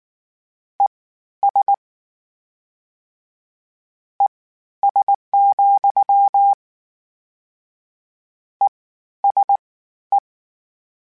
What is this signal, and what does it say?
Morse code, heard as a keyed tone, about 19 wpm, spaced as for 12 wpm, 800 Hz. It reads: ES ES, ESE
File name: General / General-mp3s / Suffixes sp @18